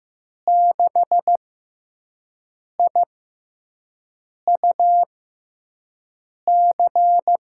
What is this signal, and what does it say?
Morse code, heard as a keyed tone, about 15 wpm, spaced as for 5 wpm, 700 Hz. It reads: 6IUC
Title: USA Call Signs - 6 Characters - 6x Character Spacing - 15wpm